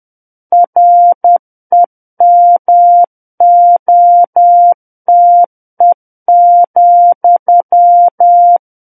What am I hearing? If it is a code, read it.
Morse code, heard as a keyed tone, about 10 wpm, 700 Hz. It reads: REMOTE,
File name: Book One / 01-Chapter 1 - 10 WPM